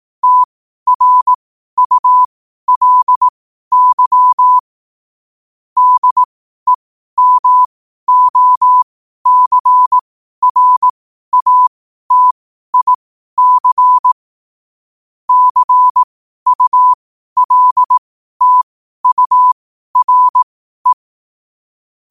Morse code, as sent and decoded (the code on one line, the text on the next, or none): - .-. ..- .-.. -.-- / -.. -- --- -.-. .-. .- - .. -.-. / -.-. ..- .-.. - ..- .-. .
TRULY DEMOCRATIC CULTURE